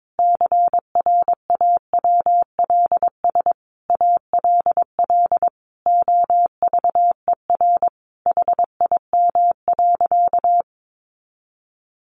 Morse code, heard as a keyed tone, about 22 wpm, 700 Hz. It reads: CRAWLS ALL OVER HIM.
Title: Additional Easy Phrases 22wpm